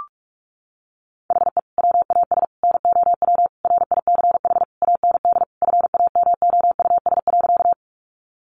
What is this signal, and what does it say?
Morse code, heard as a keyed tone, about 44 wpm, 700 Hz. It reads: HE WAS NOW RICH AND FAMOUS.
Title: Sentences from Top 1500 Words 45wpm